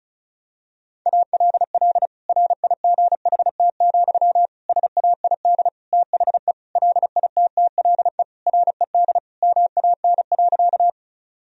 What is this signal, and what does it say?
Morse code, heard as a keyed tone, about 36 wpm, 700 Hz. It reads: ALL RIGHT, SAID THE LITTLE RED MAN.
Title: Sentences from Top 1000 Words - Review Part 2 - 35wpm